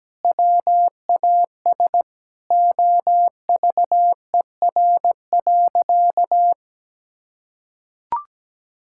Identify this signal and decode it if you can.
Morse code, heard as a keyed tone, about 17 wpm, 700 Hz. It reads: WAS OVER.
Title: Sentences from Top 1600 Words 17wpm